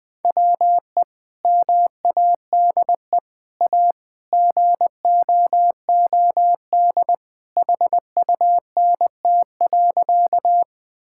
Morse code, heard as a keyed tone, about 20 wpm, 700 Hz. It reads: WE MADE A GOOD HUNT.
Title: Sentences from Top 1600 Words 20wpm